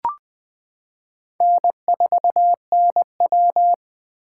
Morse code, heard as a keyed tone, about 20 wpm, 700 Hz. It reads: N4NW
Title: K1USN SST QSO Practice - 20wpm